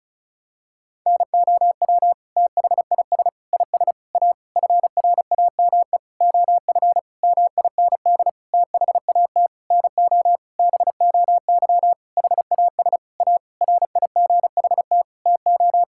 Morse code, heard as a keyed tone, about 36 wpm, 690 Hz. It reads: NOW THIS IS A FRAME OF MIND THAT NO BOY HAS A RIGHT TO
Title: Sentences from Top 1000 Words - Review Part 6 - 35wpm